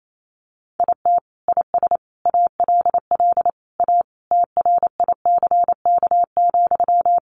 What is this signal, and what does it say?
Morse code, heard as a keyed tone, about 28 wpm, 700 Hz. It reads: IT IS ALL A TRICK,
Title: Sentences from Top 1500 Words 28wpm